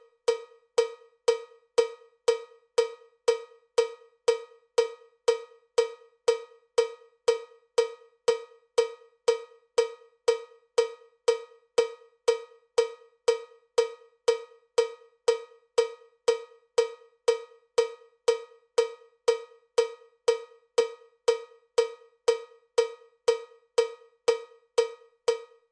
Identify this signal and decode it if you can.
There is no code.